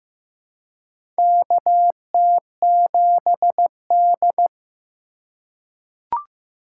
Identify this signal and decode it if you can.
Morse code, heard as a keyed tone, about 15 wpm, 700 Hz. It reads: KT7D
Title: USA Call Signs - 4 Characters - 2x Character Spacing - 15wpm